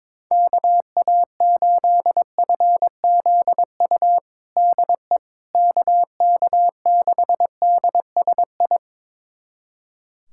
Morse code, heard as a keyed tone, about 22 wpm, 700 Hz. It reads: KA8FZU DE KK6DSI
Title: USA Callsign DE Callsign - 6 Characters - 22wpm